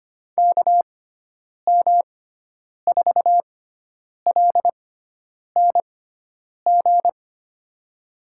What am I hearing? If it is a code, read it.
Morse code, heard as a keyed tone, about 25 wpm, spaced as for 9 wpm, 700 Hz. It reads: KM4LNG
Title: USA Call Signs - 6 Characters - 6x Character Spacing - 25wpm